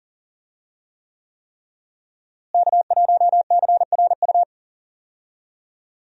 Morse code, heard as a keyed tone, about 41 wpm, 700 Hz. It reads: K1CRU
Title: USA Call Signs - 5 Characters - Repeated 3x - 40wpm